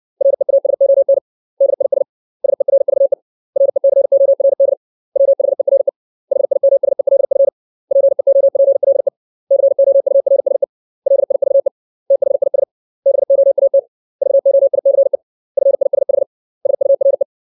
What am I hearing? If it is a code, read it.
Morse code, heard as a keyed tone, about 60 wpm, 550 Hz. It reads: REASON DIS SERVE RECORD WHERE HIMSELF PEOPLE COURSE LIVE THIS BOAT VOICE FISH SURE
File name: 632 words / 632 Words 60WPM